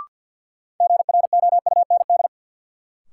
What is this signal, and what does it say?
Morse code, heard as a keyed tone, about 51 wpm, 700 Hz. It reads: GROUND
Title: Sets of 3 Words - New Word Drill - 50wpm